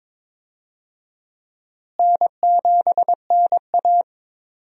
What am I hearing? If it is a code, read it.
Morse code, heard as a keyed tone, about 21 wpm, 700 Hz. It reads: N7NA
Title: USA Call Signs - 4 Characters - Repeated 3x - 22wpm